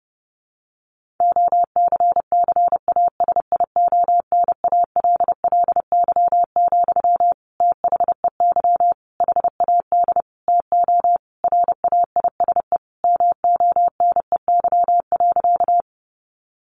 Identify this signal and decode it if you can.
Morse code, heard as a keyed tone, about 30 wpm, 700 Hz. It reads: OCCASIONALLY, THEY HAD TO RAISE MONEY.
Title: Sentences from Top 1400 Words 30wpm